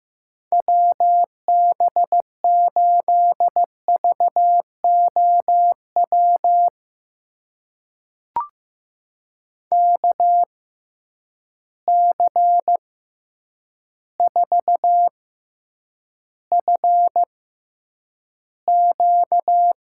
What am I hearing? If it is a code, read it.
Morse code, heard as a keyed tone, about 15 wpm, spaced as for 8 wpm, 700 Hz. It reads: WB8VOW KC4FQ
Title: USA Call Signs - 6 Characters - 6x Character Spacing - 15wpm